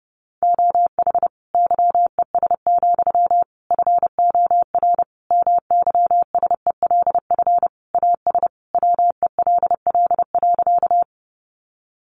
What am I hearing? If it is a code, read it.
Morse code, heard as a keyed tone, about 30 wpm, 700 Hz. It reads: OH YES, FOR MYSELF AS WELL.
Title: Sentences from Top 1300 Words 30wpm